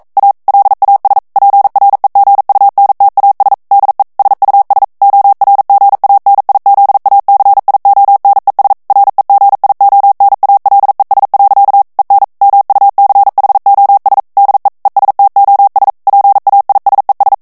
Morse code, heard as a keyed tone, about 42 wpm, 800 Hz. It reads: A LAS PREGUNTAS DE SUS ORGANIZACIONES REGIONALES. EN MUCHOS DE ESTOS PAISES
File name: morse mp3 / ESPAGNOL TEXTE EN CLAIR 1 VITESSE 40 MOTS MINUTES